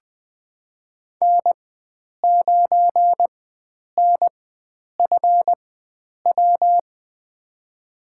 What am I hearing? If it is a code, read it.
Morse code, heard as a keyed tone, about 20 wpm, spaced as for 9 wpm, 700 Hz. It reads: N9NFW
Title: USA Call Signs - 4x Character Spacing - 20wpm